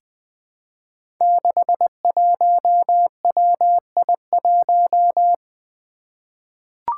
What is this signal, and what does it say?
Morse code, heard as a keyed tone, about 20 wpm, 700 Hz. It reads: 61WI1
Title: Five Challenging Letter-Number Combinations 20wpm